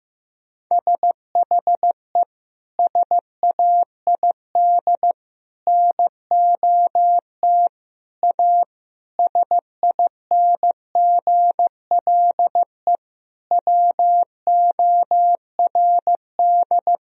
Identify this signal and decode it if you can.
Morse code, heard as a keyed tone, about 15 wpm, 700 Hz. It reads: SHE SAID NOT A SINGLE WORD
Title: Sentences from Top 1000 Words - Review Part 4 - 15wpm